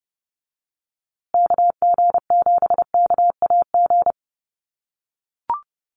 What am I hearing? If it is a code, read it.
Morse code, heard as a keyed tone, about 30 wpm, 700 Hz. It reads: KG7KAG
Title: USA Call Signs Repeated 3x 30wpm